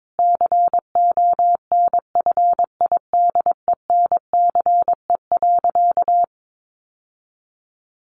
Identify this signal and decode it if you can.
Morse code, heard as a keyed tone, about 22 wpm, 700 Hz. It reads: CONFIDENCE.
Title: Sentences from Top 1500 Words 22wpm